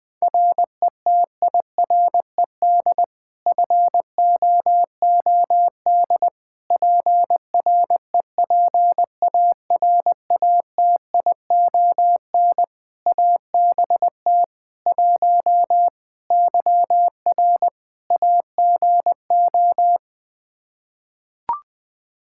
Morse code, heard as a keyed tone, about 20 wpm, 700 Hz. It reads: RETIRED FOOD PREPARATION ABT 1 YR AGO
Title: Practice QSO - Exchange 3 - More Info - 20wpm